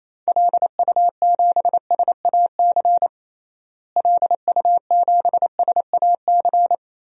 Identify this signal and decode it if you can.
Morse code, heard as a keyed tone, about 28 wpm, 700 Hz. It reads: LU7SAC LU7SAC
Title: DXCC-100 Call Signs - 6 Characters - Repeated 3x - 28wpm